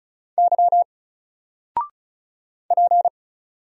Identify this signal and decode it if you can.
Morse code, heard as a keyed tone, about 36 wpm, 700 Hz. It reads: Y P
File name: Single-Letter-Number-35wpm